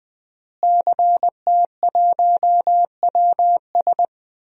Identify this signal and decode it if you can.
Morse code, heard as a keyed tone, about 20 wpm, 700 Hz. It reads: CT1WS
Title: DXCC-100 Call Signs - 5 Characters - Repeated 2x - 20wpm